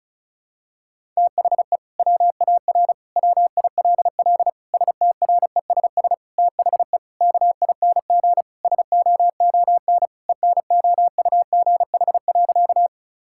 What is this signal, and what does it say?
Morse code, heard as a keyed tone, about 36 wpm, 700 Hz. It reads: THE WAR WILL STRESS THE KING SOON ENOUGH.
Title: Sentences from Top 1000 Words - Review Part 6 - 35wpm